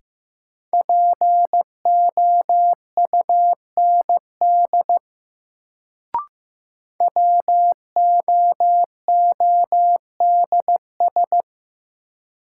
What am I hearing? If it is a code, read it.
Morse code, heard as a keyed tone, about 15 wpm, 700 Hz. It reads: POUND WOODS
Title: Sentences from Top 1000 Words - Review Part 2 - New Word Drill - 15wpm